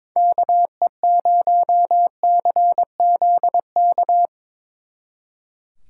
Morse code, heard as a keyed tone, about 24 wpm, 700 Hz. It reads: KE0CZK
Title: Speed Racing - USA 2x3 Call Signs 22wpm